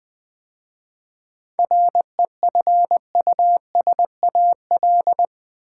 Morse code, heard as a keyed tone, about 20 wpm, 700 Hz. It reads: REFUSAL
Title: 7 Letter Words - First 3 Letters Given - 20wpm